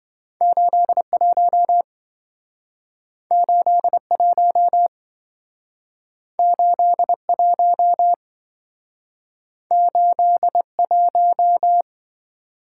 Morse code, heard as a keyed tone, about 27 wpm, 700 Hz. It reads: 81 81 81 81